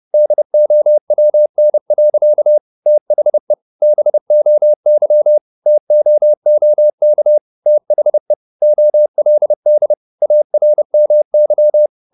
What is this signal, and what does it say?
Morse code, heard as a keyed tone, about 30 wpm, 600 Hz. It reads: DOWN. THE BOY TOOK THE OLD ARMY
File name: the old man and the sea / the old man and the sea 1 30wpm